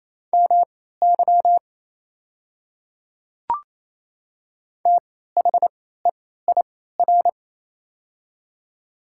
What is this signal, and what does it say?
Morse code, heard as a keyed tone, about 28 wpm, spaced as for 16 wpm, 700 Hz. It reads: MY THEIR
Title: 3x Character Spacing - Top 100 Words - 28wpm